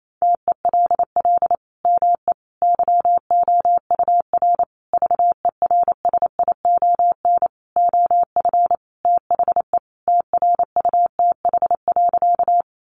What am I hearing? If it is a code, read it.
Morse code, heard as a keyed tone, about 28 wpm, 700 Hz. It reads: TELL ME YOUR VERSION OF THE TRUTH.
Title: Sentences from Top 1400 Words 28wpm